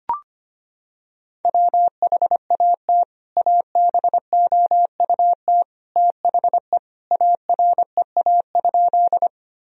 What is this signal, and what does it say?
Morse code, heard as a keyed tone, about 25 wpm, 700 Hz. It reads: WHAT ABOUT THE AREA?
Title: Sentences from Top 1000 Words - Review Part 2 - 25wpm